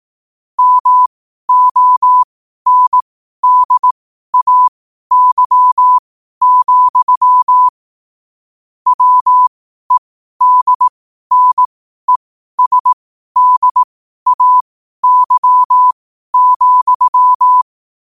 Morse code, heard as a keyed tone, about 18 wpm, spaced as for 12 wpm, 1000 Hz. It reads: MONDAY, WEDNESDAY,